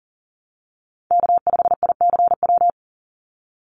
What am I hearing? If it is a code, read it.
Morse code, heard as a keyed tone, about 40 wpm, 700 Hz. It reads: K5ICW
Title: USA Call Signs 40wpm